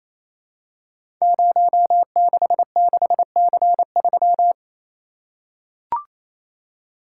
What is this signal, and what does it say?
Morse code, heard as a keyed tone, about 29 wpm, 700 Hz. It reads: 066C3
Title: Code Groups - 5 Letter:Numbers - 28wpm